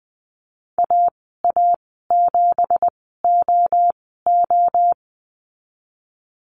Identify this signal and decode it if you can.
Morse code, heard as a keyed tone, about 19 wpm, spaced as for 14 wpm, 700 Hz. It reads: AA7OO
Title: 167 Lesson 27g - Number 7 - Call Signs 20wpm